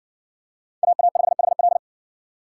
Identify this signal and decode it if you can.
Morse code, heard as a keyed tone, about 60 wpm, 700 Hz. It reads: AA5HL